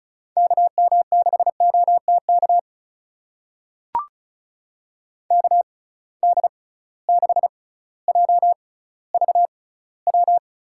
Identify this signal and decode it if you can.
Morse code, heard as a keyed tone, about 36 wpm, spaced as for 18 wpm, 700 Hz. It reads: KM6OTK KD6JVW
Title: USA Call Signs - 6 Characters - 6x Character Spacing - 35wpm